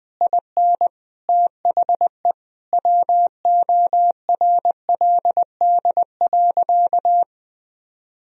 Morse code, heard as a keyed tone, about 20 wpm, 700 Hz. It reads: IN THE WORLD.
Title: Sentences from Top 1000 Words - Review Part 3 - 20wpm